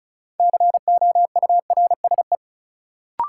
Morse code, heard as a keyed tone, about 36 wpm, 700 Hz. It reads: COURSE
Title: Sentences from Top 500 Words - New Word Drill - 35wpm